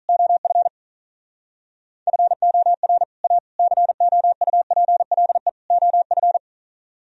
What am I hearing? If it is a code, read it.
Morse code, heard as a keyed tone, about 44 wpm, 700 Hz. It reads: OF FOR A COUPLE OF